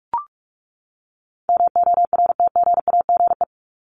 Morse code, heard as a keyed tone, about 44 wpm, 700 Hz. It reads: MORTGAGE
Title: Top 1751-2000 Words 45wpm